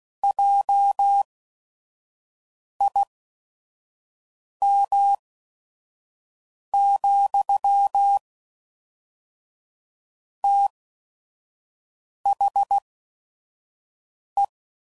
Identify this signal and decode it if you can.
Morse code, heard as a keyed tone, about 16 wpm, spaced as for 5 wpm, 800 Hz. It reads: JIM, THE
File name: CW Code Groups @5WPM / QSOs / WD0FBT 1